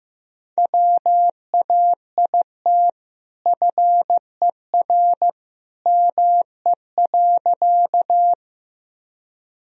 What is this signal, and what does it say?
Morse code, heard as a keyed tone, about 15 wpm, 700 Hz. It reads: WAIT FER ME.
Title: Sentences from Top 500 Words and Abbreviations 15wpm